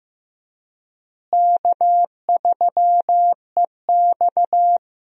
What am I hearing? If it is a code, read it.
Morse code, heard as a keyed tone, about 15 wpm, 700 Hz. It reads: K3EX
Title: USA Call Signs - 4 Characters - Repeated 3x - 15wpm